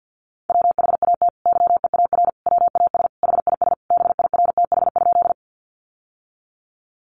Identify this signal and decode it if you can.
Morse code, heard as a keyed tone, about 52 wpm, 700 Hz. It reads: WHAT YEAR WAS HIS BIRTH?
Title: Sentences from Top 800 Words 50wpm